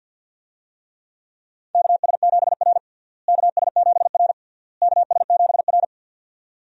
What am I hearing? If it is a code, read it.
Morse code, heard as a keyed tone, about 50 wpm, 700 Hz. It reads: KS7R KS7R KS7R
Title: USA Call Signs - 4 Characters - Repeated 3x - 50wpm